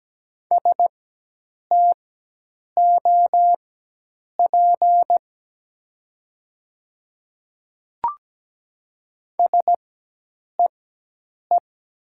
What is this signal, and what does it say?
Morse code, heard as a keyed tone, about 17 wpm, spaced as for 8 wpm, 700 Hz. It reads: STOP SEE
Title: 4x Character Spacing - Top 300 Words - 17wpm